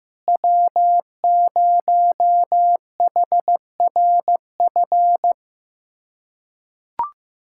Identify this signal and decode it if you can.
Morse code, heard as a keyed tone, about 15 wpm, 700 Hz. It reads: W0HRF
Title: USA Callsign DE Callsign - 5 Characters - 15wpm